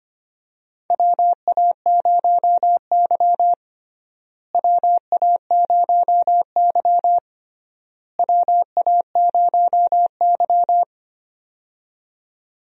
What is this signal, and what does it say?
Morse code, heard as a keyed tone, about 25 wpm, 700 Hz. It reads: WA0Y WA0Y WA0Y
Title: USA Call Signs - 4 Characters - Repeated 3x - 25wpm